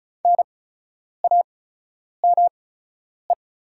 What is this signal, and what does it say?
Morse code, heard as a keyed tone, about 36 wpm, spaced as for 10 wpm, 700 Hz. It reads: NAME
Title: 8x Character Spacing - Top 200 Words - 35wpm